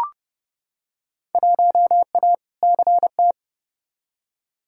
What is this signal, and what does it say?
Morse code, heard as a keyed tone, about 29 wpm, 700 Hz. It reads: 1A CT
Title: Field Day Exchanges Encore - 30wpm